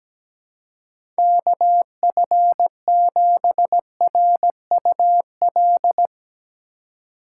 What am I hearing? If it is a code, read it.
Morse code, heard as a keyed tone, about 17 wpm, 700 Hz. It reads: KF7RUL
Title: USA Call Signs - 6 Characters - 4x Character Spacing - 17wpm